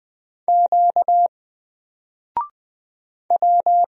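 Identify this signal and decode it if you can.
Morse code, heard as a keyed tone, about 20 wpm, 700 Hz. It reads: Q W
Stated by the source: Single letter and number Morse drill at 20 wpm, from N4LCD.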